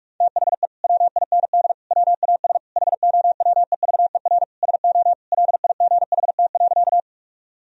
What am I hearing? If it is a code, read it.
Morse code, heard as a keyed tone, about 47 wpm, 700 Hz. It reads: THE WIND WAS HOWEVER SO LIGHT.
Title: Sentences from Top 1000 Words - Review Part 2 - 45wpm